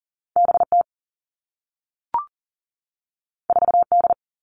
Morse code, heard as a keyed tone, about 42 wpm, 700 Hz. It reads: B T 4 D
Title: Two Challenging Letter-Number Combinations 40wpm